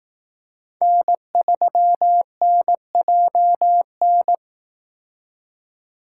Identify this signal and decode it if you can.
Morse code, heard as a keyed tone, about 18 wpm, 700 Hz. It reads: N3NJN
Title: Speed Racing - USA 1x3 Call Signs 15wpm